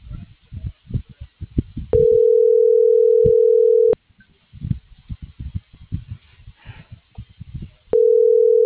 Ringback tone (North America), 440 + 480 Hz.